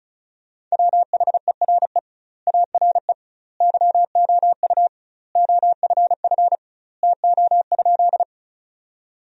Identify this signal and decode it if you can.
Morse code, heard as a keyed tone, about 34 wpm, 700 Hz. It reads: WHERE ARE YOU OFF TO?
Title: 2x Word-Spacing - Sentences from Top 200 Words - 35wpm